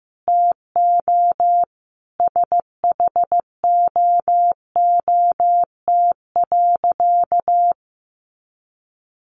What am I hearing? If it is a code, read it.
Morse code, heard as a keyed tone, about 15 wpm, 700 Hz. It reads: TO SHOOT.
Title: Sentences from Top 1200 Words 15wpm